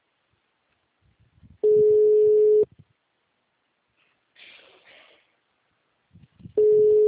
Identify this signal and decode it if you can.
Ringback tone (Europe, 425 Hz).
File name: out- -2035- - - .32